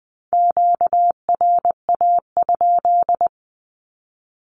Morse code, heard as a keyed tone, about 20 wpm, 700 Hz. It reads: QRA?
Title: Q-Codes with Spoken Meaning - 20wpm